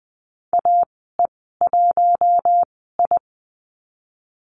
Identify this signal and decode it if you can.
Morse code, heard as a keyed tone, about 20 wpm, spaced as for 15 wpm, 700 Hz. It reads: AE1I
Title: 082 Lesson 14g - Number 2 - Call Signs 20wpm